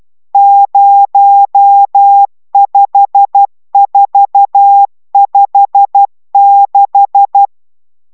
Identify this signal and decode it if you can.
Morse code, heard as a keyed tone, about 12 wpm, 800 Hz. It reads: 05456
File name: taviro tanulas / cw 69 20 10